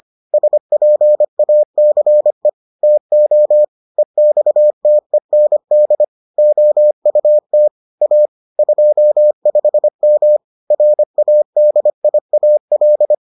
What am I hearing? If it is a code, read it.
Morse code, heard as a keyed tone, about 25 wpm, 600 Hz. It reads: SPACE TO EXTEND OUT A 25M RADIAL